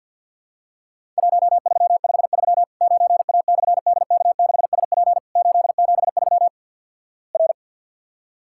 Morse code, heard as a keyed tone, about 50 wpm, 700 Hz. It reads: 1353 9A/DK6SP 873 R
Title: Worked All Europe DX Contest - QTC Exchange with Repeats - 50wpm